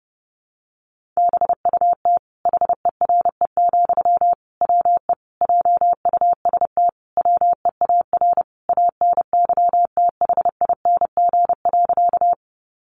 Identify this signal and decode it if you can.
Morse code, heard as a keyed tone, about 28 wpm, 700 Hz. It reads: BUT HERE, WE JUST WEAR ANYTHING.
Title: Sentences from Top 1500 Words 30wpm